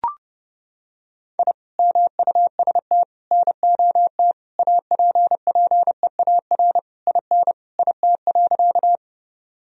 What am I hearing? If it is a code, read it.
Morse code, heard as a keyed tone, about 32 wpm, 700 Hz. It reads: I MUST NOT APPEAR IN IT.
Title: Sentences from Top 1000 Words - Review Part 2 - 30wpm